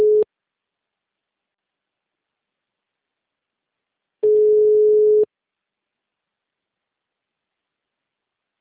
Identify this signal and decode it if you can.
Ringback tone (Europe, 425 Hz).
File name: out- -2009